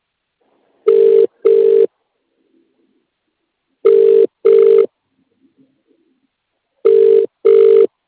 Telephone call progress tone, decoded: ringback tone